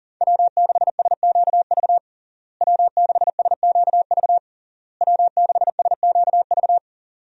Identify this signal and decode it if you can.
Morse code, heard as a keyed tone, about 40 wpm, 700 Hz. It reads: W6SQV W6SQV W6SQV
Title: USA Call Signs - 5 Characters - Repeated 3x - 40wpm